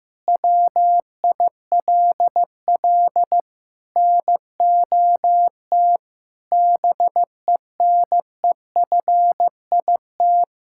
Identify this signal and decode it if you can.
Morse code, heard as a keyed tone, about 15 wpm, 700 Hz. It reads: WILL NOT BENEFIT
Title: Sentences from Top 1000 Words - Review Part 6 - 15wpm